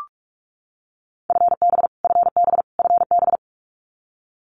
Morse code, heard as a keyed tone, about 44 wpm, 700 Hz. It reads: FB FB FB
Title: Abbreviations Repeated 3x - 45wpm